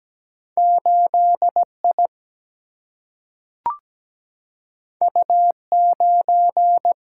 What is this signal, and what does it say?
Morse code, heard as a keyed tone, about 17 wpm, 700 Hz. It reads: 8 I U 9
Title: Single-Letter Single-Number 17wpm